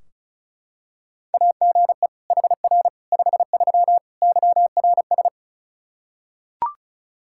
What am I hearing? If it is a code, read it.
Morse code, heard as a keyed tone, about 34 wpm, 700 Hz. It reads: AGE HR 53 YRS
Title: Practice QSO - Exchange 3 - More Info - 35wpm